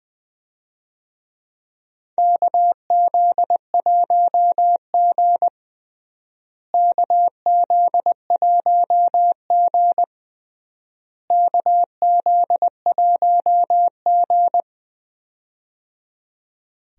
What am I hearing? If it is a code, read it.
Morse code, heard as a keyed tone, about 20 wpm, 700 Hz. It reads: KZ1G KZ1G KZ1G
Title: USA Call Signs - 4 Characters - Repeated 3x - 20wpm